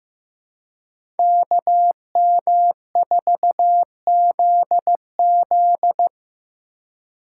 Morse code, heard as a keyed tone, about 15 wpm, 700 Hz. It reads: KM4ZZ
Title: K1USN SST QSO Practice - 15wpm